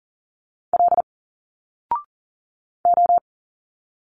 Morse code, heard as a keyed tone, about 38 wpm, 700 Hz. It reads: L O
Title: Single Letters 40wpm